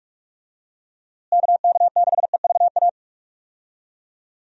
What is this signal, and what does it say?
Morse code, heard as a keyed tone, about 46 wpm, 700 Hz. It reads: KK6EVA
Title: USA Call Signs - 6 Characters - Repeated 2x - 45wpm